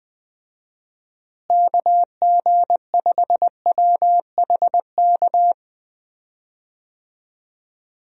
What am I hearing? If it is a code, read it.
Morse code, heard as a keyed tone, about 21 wpm, 700 Hz. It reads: KG5WHK
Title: USA Call Signs - 6 Characters - Repeated 2x - 20wpm